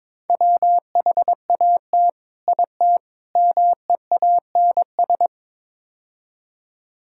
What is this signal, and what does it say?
Morse code, heard as a keyed tone, about 22 wpm, 700 Hz. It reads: WHAT IT MEANS
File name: practice / 22wpm / Build A Phrase - 5-Word Phrases - 22wpm